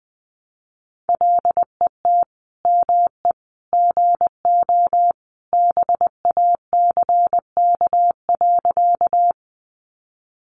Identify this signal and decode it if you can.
Morse code, heard as a keyed tone, about 20 wpm, 700 Hz. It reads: LET ME GO BACK.
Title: Sentences from Top 300 Words 20wpm